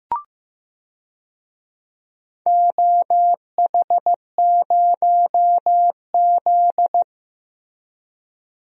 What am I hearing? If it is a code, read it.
Morse code, heard as a keyed tone, about 15 wpm, 700 Hz. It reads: OH0Z
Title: DXCC-100 Call Signs - 4 Characters - Repeated 2x - 15wpm